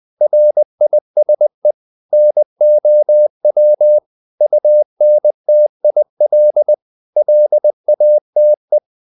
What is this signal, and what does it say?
Morse code, heard as a keyed tone, about 20 wpm, 600 Hz. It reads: RISE NOW UNTIL LATE
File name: the old man and the sea part 2 20wpm